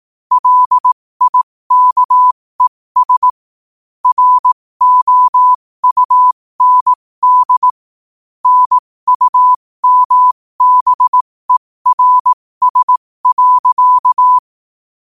Morse code, heard as a keyed tone, about 18 wpm, spaced as for 15 wpm, 1000 Hz. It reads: LIKES ROUND NUMBERS.
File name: qotdfiles / 15wpm